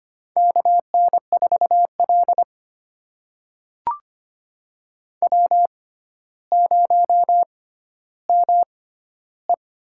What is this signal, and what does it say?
Morse code, heard as a keyed tone, about 25 wpm, spaced as for 13 wpm, 700 Hz. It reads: KN4L W0ME
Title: USA Call Signs - 4 Characters - 6x Character Spacing - 25wpm